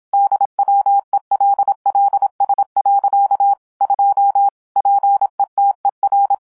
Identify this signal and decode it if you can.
Morse code, heard as a keyed tone, about 26 wpm, 800 Hz. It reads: DWELLS. 2 PETER